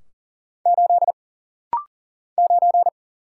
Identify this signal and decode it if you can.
Morse code, heard as a keyed tone, about 40 wpm, 700 Hz. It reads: 8 9
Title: Single Letter-Number - Rapid-Fire 40wpm